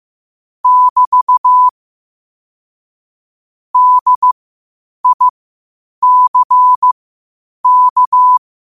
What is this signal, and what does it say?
Morse code, heard as a keyed tone, about 15 wpm, spaced as for 8 wpm, 1000 Hz. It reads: <BT> DICK
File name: qotdfiles / 8wpm